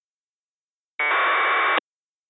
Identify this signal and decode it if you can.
Packet radio (AX.25, as in APRS).